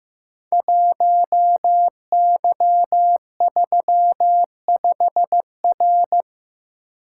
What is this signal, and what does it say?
Morse code, heard as a keyed tone, about 15 wpm, 700 Hz. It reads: 1Y35R
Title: Code Groups - 5 Letter:Numbers - 15wpm